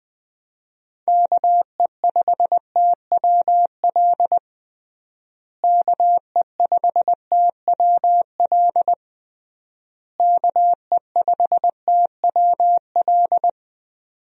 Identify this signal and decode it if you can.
Morse code, heard as a keyed tone, about 20 wpm, 700 Hz. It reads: KE5TWL KE5TWL KE5TWL
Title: USA Call Signs - 6 Characters - Repeated 3x - 20wpm